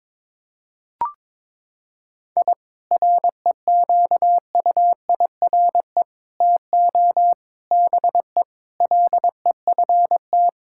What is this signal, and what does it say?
Morse code, heard as a keyed tone, about 22 wpm, 700 Hz. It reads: I REQUIRE TO BE LEFT
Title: Sentences from Top 1000 Words - Review Part 6 - 22wpm